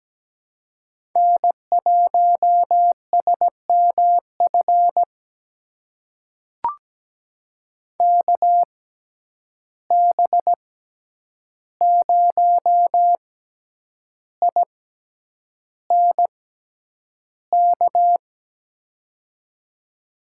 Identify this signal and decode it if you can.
Morse code, heard as a keyed tone, about 17 wpm, spaced as for 8 wpm, 700 Hz. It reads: N1SMF KB0INK
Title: USA Call Signs - 6x Character Spacing - 17wpm